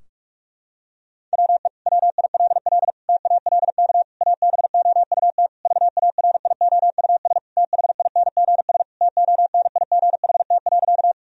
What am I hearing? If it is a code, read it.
Morse code, heard as a keyed tone, about 46 wpm, 700 Hz. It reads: WE WILL TALK ABOUT VARIOUS THINGS TONIGHT.
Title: Sentences from Top 1000 Words - Review Part 4 - 45wpm